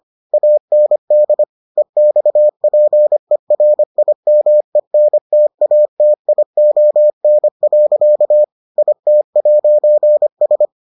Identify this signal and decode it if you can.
Morse code, heard as a keyed tone, about 25 wpm, 600 Hz. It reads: AND EXPERIMENTATION. IT'S